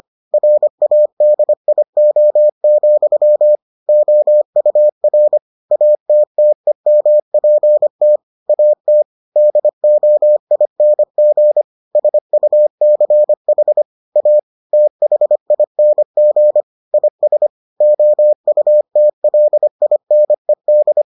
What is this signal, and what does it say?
Morse code, heard as a keyed tone, about 25 wpm, 600 Hz. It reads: RADIO, OUR ATTEMPT AT DOING SUCH A THING IS OUTLINED